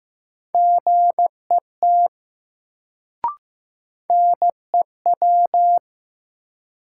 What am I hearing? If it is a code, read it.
Morse code, heard as a keyed tone, about 15 wpm, 700 Hz. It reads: GET NEW
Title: Sets of 2 Words - New Word Drill - 15wpm